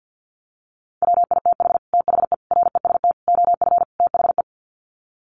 Morse code, heard as a keyed tone, about 50 wpm, 700 Hz. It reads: WITH THE REST OF THE
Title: Sets of 5 Words 50wpm